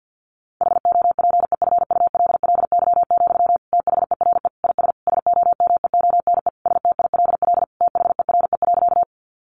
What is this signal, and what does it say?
Morse code, heard as a keyed tone, about 50 wpm, 700 Hz. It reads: HOPEFULLY, THERE IS SOMEONE STILL THERE.
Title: Sentences from Top 1500 Words 50wpm